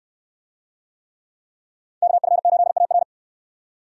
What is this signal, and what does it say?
Morse code, heard as a keyed tone, about 69 wpm, 700 Hz. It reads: KF8AR